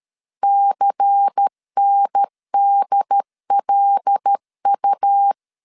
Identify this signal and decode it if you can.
Morse code, heard as a keyed tone, about 13 wpm, 800 Hz. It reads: CNDLU
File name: CW / DET / 1 6 1500C S A 12wpm 3440